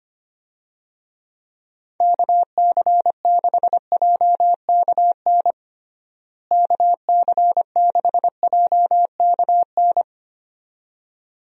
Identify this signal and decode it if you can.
Morse code, heard as a keyed tone, about 25 wpm, 700 Hz. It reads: KC6JKN KC6JKN
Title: USA Call Signs - 6 Characters - Repeated 2x - 25wpm